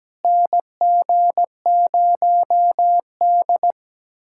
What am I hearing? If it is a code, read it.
Morse code, heard as a keyed tone, about 17 wpm, 700 Hz. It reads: NG0D
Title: USA Callsign DE Callsign - 4 Characters - 17wpm